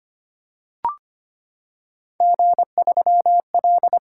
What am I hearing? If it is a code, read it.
Morse code, heard as a keyed tone, about 25 wpm, 700 Hz. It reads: G3L
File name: Three Letter-Number Combinations - 25wpm